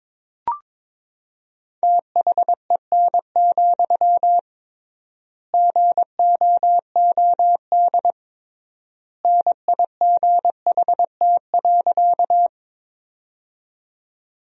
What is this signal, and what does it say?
Morse code, heard as a keyed tone, about 22 wpm, 700 Hz. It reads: THEN, GOOD NIGHT.
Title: 3x Word-Spacing - Sentences from Top 300 Words - 22wpm